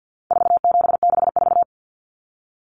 Morse code, heard as a keyed tone, about 50 wpm, 700 Hz. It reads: 4764
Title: Numbers - 4-Digit 50wpm